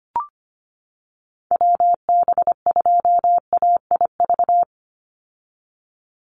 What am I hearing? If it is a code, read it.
Morse code, heard as a keyed tone, about 25 wpm, 700 Hz. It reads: WB2AIV
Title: Call Signs - 25wpm